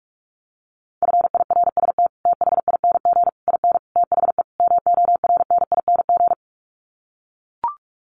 Morse code, heard as a keyed tone, about 44 wpm, 700 Hz. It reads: FIRST THING IN THE MORNING
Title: Sets of 5 Words Encore2 45wpm